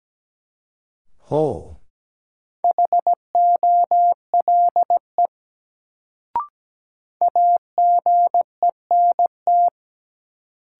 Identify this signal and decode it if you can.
Morse code, heard as a keyed tone, about 17 wpm, 700 Hz. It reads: HOLE AGENT
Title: Sentences from Top 1600 Words - New Word Drill Alt - 17wpm